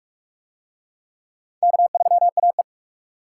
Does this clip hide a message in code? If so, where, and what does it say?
Yes: Morse code from 1.6 to 2.6 s, heard as a keyed tone, about 44 wpm, 700 Hz. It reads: K3AE